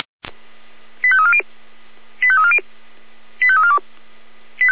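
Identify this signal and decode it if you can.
Five-tone selective call (ZVEI), heard as quick runs of five tones, tones of about 70 ms. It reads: 85340 85340 85342
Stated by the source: ZVEI five-tone call received by SDR.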